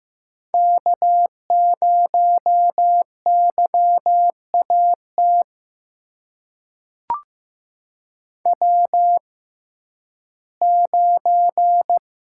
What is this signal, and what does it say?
Morse code, heard as a keyed tone, about 15 wpm, spaced as for 11 wpm, 700 Hz. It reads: K0YAT W9
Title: USA Call Signs - 5 Characters - 6x Character Spacing - 15wpm